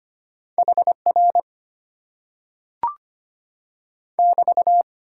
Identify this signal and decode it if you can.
Morse code, heard as a keyed tone, about 25 wpm, 700 Hz. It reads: HR <BT>
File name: Top 90 QSO Elements - 8x Character Spacing - 25wpm